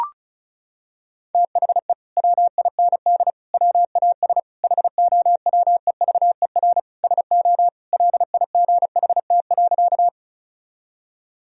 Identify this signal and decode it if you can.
Morse code, heard as a keyed tone, about 36 wpm, 700 Hz. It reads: THE WIND WAS HOWEVER SO LIGHT.
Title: Sentences from Top 1000 Words - Review Part 2 - 35wpm